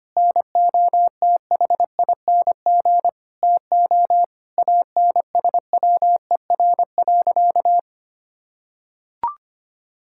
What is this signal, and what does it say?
Morse code, heard as a keyed tone, about 25 wpm, 700 Hz. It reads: NOTHING TO ANSWER.
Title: Sentences from Top 1600 Words 25wpm